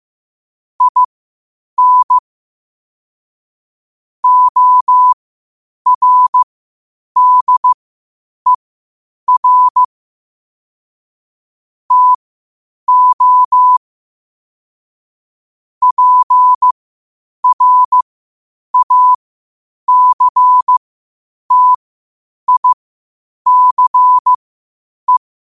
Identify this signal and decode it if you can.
Morse code, heard as a keyed tone, about 15 wpm, spaced as for 8 wpm, 1000 Hz. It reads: IN ORDER TO PRACTICE
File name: qotdfiles / 8wpm